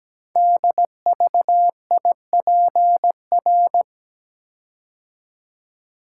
Morse code, heard as a keyed tone, about 17 wpm, 700 Hz. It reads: DVIPR
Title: Code Groups - 5 Letters - 17wpm